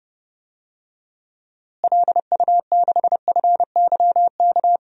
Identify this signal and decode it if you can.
Morse code, heard as a keyed tone, about 29 wpm, 700 Hz. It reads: LU6FYK
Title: DXCC-100 Call Signs - 6 Characters - Repeated 3x - 30wpm